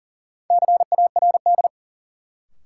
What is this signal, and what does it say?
Morse code, heard as a keyed tone, about 40 wpm, 700 Hz. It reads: CARD
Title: Sentences from Top 600 Words - New Word Drill - 40wpm